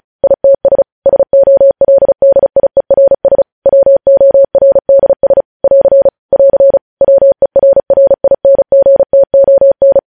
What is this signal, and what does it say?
Morse code, heard as a keyed tone, about 36 wpm, 550 Hz. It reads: ITS SOLDIERS WORDS <AR> <AR> WERRINGTON
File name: podcast / News headlines-1x-35x35